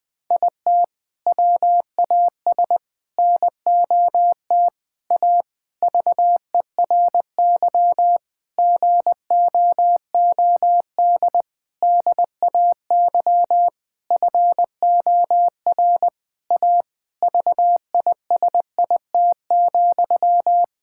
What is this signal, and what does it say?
Morse code, heard as a keyed tone, about 21 wpm, 700 Hz. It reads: IT WAS NOT A VERY GOOD DAY FOR A VISIT,
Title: Sentences from Top 1000 Words - Review Part 6 - 20wpm